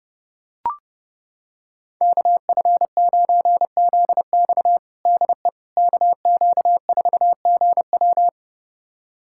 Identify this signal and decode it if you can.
Morse code, heard as a keyed tone, about 30 wpm, 700 Hz. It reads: KF9ZX DE KQ4GW
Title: USA Callsign DE Callsign - 5 Characters - 30wpm